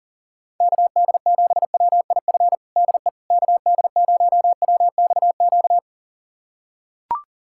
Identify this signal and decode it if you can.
Morse code, heard as a keyed tone, about 40 wpm, 700 Hz. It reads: KD7WIF DE KD0WXQ
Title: USA Callsign DE Callsign - 6 Characters - 40wpm